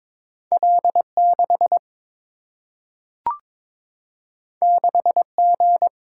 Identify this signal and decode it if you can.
Morse code, heard as a keyed tone, about 22 wpm, 700 Hz. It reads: L6 6G
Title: Single-Letter Single-Number 22wpm